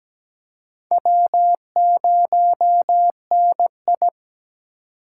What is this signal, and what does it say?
Morse code, heard as a keyed tone, about 17 wpm, 700 Hz. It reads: W0NI
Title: USA Call Signs - 4 Characters - 2x Character Spacing - 17wpm